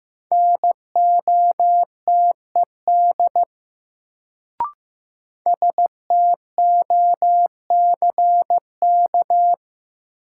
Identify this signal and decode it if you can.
Morse code, heard as a keyed tone, about 16 wpm, 700 Hz. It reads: NOTED STOCK
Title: Sets of 5 Words - New Word Drill - 15wpm